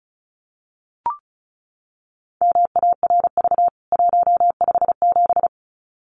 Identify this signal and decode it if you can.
Morse code, heard as a keyed tone, about 34 wpm, 700 Hz. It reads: MARV 157
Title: CWT Exchange 35wpm